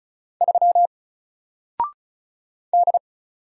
Morse code, heard as a keyed tone, about 35 wpm, 700 Hz. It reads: 3 D
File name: Single-Letter-Number-35wpm